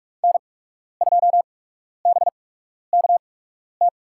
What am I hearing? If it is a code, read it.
Morse code, heard as a keyed tone, about 44 wpm, spaced as for 12 wpm, 690 Hz. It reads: N2BKT
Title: USA Call Signs - 5 Characters - 8x Character Spacing - 45wpm